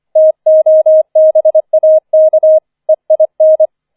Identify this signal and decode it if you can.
Morse code, heard as a keyed tone, about 25 wpm, 630 Hz. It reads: TOBAK EIN